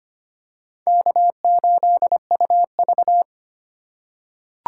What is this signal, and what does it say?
Morse code, heard as a keyed tone, about 25 wpm, 700 Hz. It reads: K8UV
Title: USA Call Signs - 4 Characters - 3x Character Spacing - 25wpm